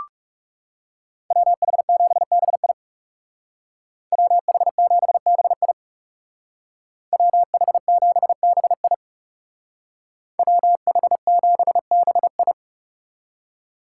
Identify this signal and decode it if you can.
Morse code, heard as a keyed tone, about 36 wpm, 700 Hz. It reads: WH7BI WH7BI WH7BI WH7BI
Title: Speed Racing - USA 2x2 Call Signs 30wpm